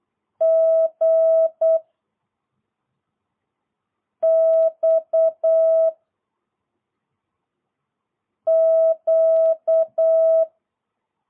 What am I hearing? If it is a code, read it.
Morse code, heard as a keyed tone, about 8 wpm, 660 Hz. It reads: G X Q